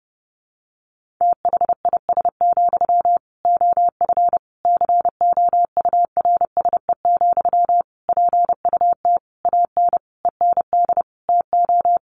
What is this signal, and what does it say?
Morse code, heard as a keyed tone, about 30 wpm, 700 Hz. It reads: THIS, OF COURSE, PUT AN END TO
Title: Sentences from Top 1400 Words 30wpm